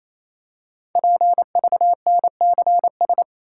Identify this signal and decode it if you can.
Morse code, heard as a keyed tone, about 27 wpm, 700 Hz. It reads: PVNCS
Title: Code Groups - 5 Letter:Numbers - 28wpm